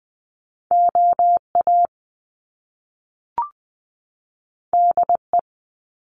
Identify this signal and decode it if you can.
Morse code, heard as a keyed tone, about 20 wpm, 700 Hz. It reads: O A D E
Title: Top Two Letter Combinations in Words 20wpm